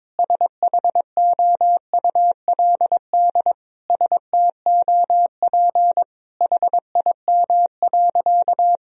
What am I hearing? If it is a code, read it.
Morse code, heard as a keyed tone, about 22 wpm, 700 Hz. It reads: SHOULD STOP HIM.
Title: Sentences from Top 1000 Words - Review Part 1 - 22wpm